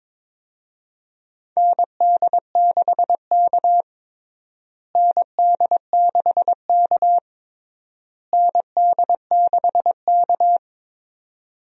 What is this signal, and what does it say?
Morse code, heard as a keyed tone, about 22 wpm, 700 Hz. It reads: ND6K ND6K ND6K
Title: USA Call Signs - 4 Characters - Repeated 3x - 22wpm